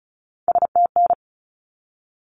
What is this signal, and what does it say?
Morse code, heard as a keyed tone, about 34 wpm, 700 Hz. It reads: STN
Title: Abbreviations 35wpm